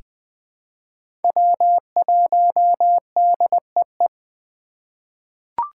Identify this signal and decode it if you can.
Morse code, heard as a keyed tone, about 20 wpm, 700 Hz. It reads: W1DEE
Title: USA Call Signs - 5 Characters - 2x Character Spacing - 20wpm